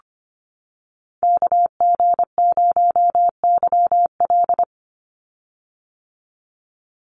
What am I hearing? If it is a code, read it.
Morse code, heard as a keyed tone, about 25 wpm, 700 Hz. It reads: KG0YL
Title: Call Signs - 25wpm